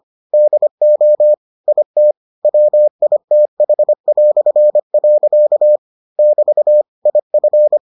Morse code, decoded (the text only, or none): DO IT WITH". <BT> IF